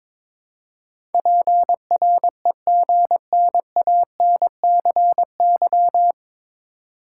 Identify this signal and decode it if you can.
Morse code, heard as a keyed tone, about 21 wpm, 700 Hz. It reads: PREGNANCY
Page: Morse Code Ninja practice 22wpm